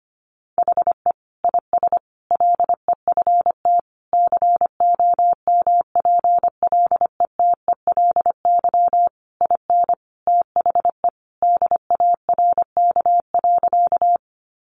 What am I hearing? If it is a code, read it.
Morse code, heard as a keyed tone, about 25 wpm, 700 Hz. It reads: HE IS LEFT COMPLETELY IN THE DARK.
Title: Sentences from Top 1100 Words 25wpm